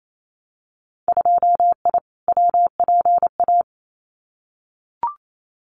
Morse code, heard as a keyed tone, about 28 wpm, 700 Hz. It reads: 2I WPA
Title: Winter Field Day Exchanges 28wpm